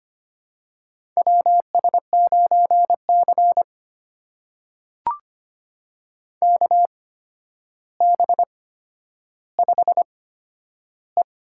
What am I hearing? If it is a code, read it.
Morse code, heard as a keyed tone, about 25 wpm, spaced as for 11 wpm, 700 Hz. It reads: WS9C KB5E
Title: USA Call Signs - 8x Character Spacing - 25wpm